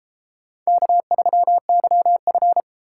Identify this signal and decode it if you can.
Morse code, heard as a keyed tone, about 31 wpm, 700 Hz. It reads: K3YF